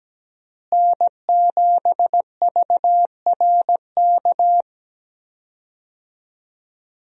Morse code, heard as a keyed tone, about 17 wpm, 700 Hz. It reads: N7VRK
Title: USA Call Signs - 5 Characters - Repeated 2x - 17wpm